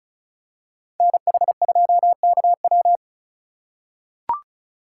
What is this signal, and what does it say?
Morse code, heard as a keyed tone, about 34 wpm, 700 Hz. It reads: NH2KW
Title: USA Call Signs - 5 Characters - 6x Character Spacing - 35wpm